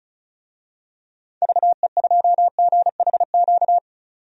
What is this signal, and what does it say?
Morse code, heard as a keyed tone, about 36 wpm, 690 Hz. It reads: VE2GHQ
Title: DXCC-100 Call Signs - 6 Characters - 35wpm